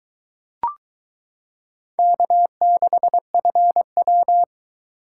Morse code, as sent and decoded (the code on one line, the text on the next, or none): -.- -.... ..-. .--
K6FW